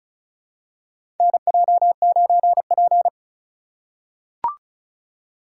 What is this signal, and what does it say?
Morse code, heard as a keyed tone, about 36 wpm, 700 Hz. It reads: NJ9P
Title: USA Call Signs - 4 Characters - 2x Character Spacing - 35wpm